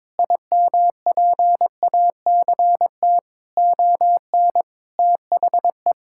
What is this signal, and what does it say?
Morse code, heard as a keyed tone, about 22 wpm, 700 Hz. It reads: IMPACT ON THE
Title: Sentences from Top 1000 Words - Review Part 6 - 22wpm